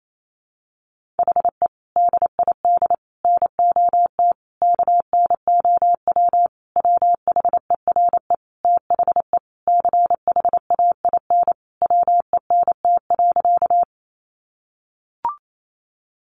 Morse code, heard as a keyed tone, about 28 wpm, 700 Hz. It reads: HE DID NOT KNOW WHERE THE CHAIN WENT.
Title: Sentences from Top 1500 Words 28wpm